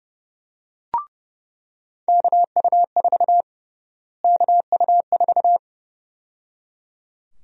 Morse code, heard as a keyed tone, about 29 wpm, 700 Hz. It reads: KU4 KU4
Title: USA Call Sign Prefixes - Repeated 2x - 30wpm